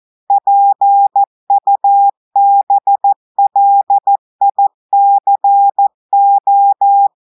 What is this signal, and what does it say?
Morse code, heard as a keyed tone, about 14 wpm, 800 Hz. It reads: PUBLICO
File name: audios / portugais / portugais texte en clair 2 vitesse 14 mots minutes